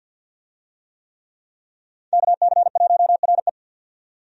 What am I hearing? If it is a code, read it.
Morse code, heard as a keyed tone, about 52 wpm, 700 Hz. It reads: KC1RE
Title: USA Call Signs - 5 Characters - 50wpm